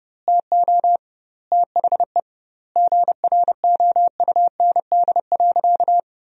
Morse code, heard as a keyed tone, about 32 wpm, 700 Hz. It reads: TO THE GROUND.
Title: 2x Word-Spacing - Sentences from Top 500 Words - 30wpm